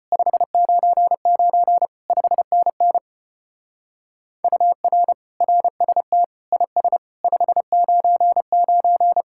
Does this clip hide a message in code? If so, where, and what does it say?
Yes: Morse code from 0.1 to 9.2 s, heard as a keyed tone, about 32 wpm, 700 Hz. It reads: 599 5NN UR RST IS 599